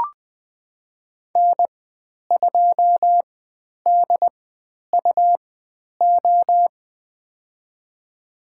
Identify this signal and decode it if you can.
Morse code, heard as a keyed tone, about 20 wpm, spaced as for 10 wpm, 700 Hz. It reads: N2DUO
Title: 102 Lesson 17g - Letter u - Call Signs 20wpm